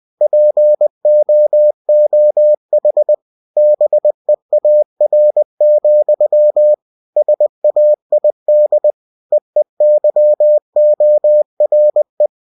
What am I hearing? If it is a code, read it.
Morse code, heard as a keyed tone, about 20 wpm, 600 Hz. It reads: POOH BEAR, SAID EEYORE
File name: cw-books / winnie-the-pooh / winnie-20wpm-page-0045